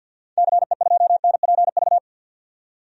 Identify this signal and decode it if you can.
Morse code, heard as a keyed tone, about 46 wpm, 700 Hz. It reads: CE2NPV
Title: DXCC-100 Call Signs - 6 Characters - 50wpm